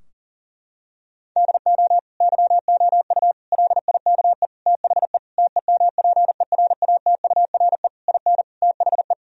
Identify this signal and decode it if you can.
Morse code, heard as a keyed tone, about 40 wpm, 700 Hz. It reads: DO YOU LIKE THE TEMPERATURE IN THE
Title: Sentences from Top 1000 Words - Review Part 4 - 40wpm